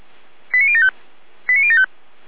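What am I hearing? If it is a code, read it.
Five-tone selective call (ZVEI), heard as quick runs of five tones, tones of about 70 ms. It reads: 89075 89075
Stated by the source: ZVEI five-tone selective call received by SDR, from the Feuerwehr Orlamünde audio archive.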